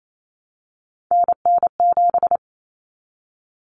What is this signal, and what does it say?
Morse code, heard as a keyed tone, about 28 wpm, 700 Hz. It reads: NN7
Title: Common Cut Numbers - 3-Digit 28wpm